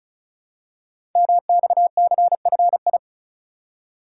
Morse code, heard as a keyed tone, about 36 wpm, 700 Hz. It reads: MXCFI
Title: Code Groups - 5 Letters - 35wpm